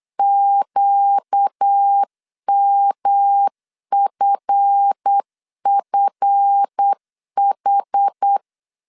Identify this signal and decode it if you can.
Morse code, heard as a keyed tone, about 8 wpm, 800 Hz. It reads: QMFFH
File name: LEZIONI / 7 GF9Z / 7 GF 8wpm 3108